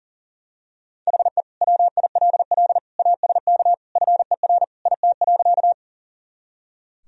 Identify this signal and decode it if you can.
Morse code, heard as a keyed tone, about 40 wpm, 700 Hz. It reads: HE WILL ASK FER IT.
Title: Sentences from Top 200 Words and Abbreviations 40wpm